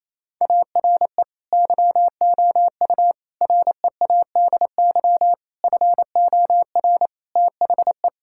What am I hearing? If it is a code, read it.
Morse code, heard as a keyed tone, about 28 wpm, 700 Hz. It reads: ARE YOU READY FOR THE
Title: Sentences from Top 1000 Words - Review Part 6 - 28wpm